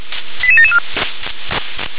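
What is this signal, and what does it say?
Five-tone selective call (ZVEI), heard as quick runs of five tones, tones of about 70 ms. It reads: 88774
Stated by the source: ZVEI five-tone call 88773 received by SDR.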